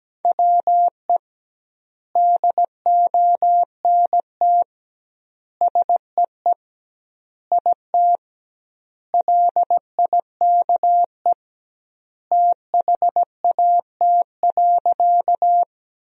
Morse code, heard as a keyed tone, about 17 wpm, 700 Hz. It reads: WE DONT SEE IT LIKE THAT.
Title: 2x Word-Spacing - Sentences from Top 300 Words - 17wpm